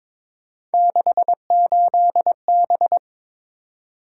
Morse code, heard as a keyed tone, about 22 wpm, 700 Hz. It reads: 68B
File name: practice / 22wpm / Three Letter-Number Combinations - 22wpm